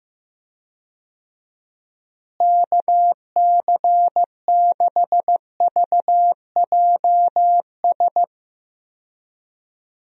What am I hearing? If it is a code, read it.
Morse code, heard as a keyed tone, about 15 wpm, 700 Hz. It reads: KC6VJS
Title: USA Call Signs - 6 Characters - Repeated 2x - 15wpm